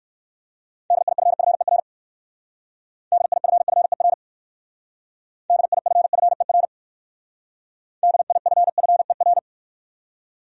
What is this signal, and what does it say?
Morse code, heard as a keyed tone, about 58 wpm, 700 Hz. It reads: DIFFER DIFFER DIFFER DIFFER